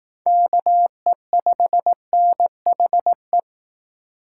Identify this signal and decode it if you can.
Morse code, heard as a keyed tone, about 18 wpm, 700 Hz. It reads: KE5NHE
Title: Speed Racing - USA 2x3 Call Signs 15wpm